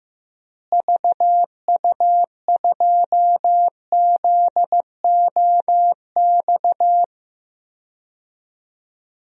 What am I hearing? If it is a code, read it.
Morse code, heard as a keyed tone, about 15 wpm, 700 Hz. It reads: VU2ZOX
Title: DXCC-100 Call Signs - 6 Characters - Repeated 2x - 15wpm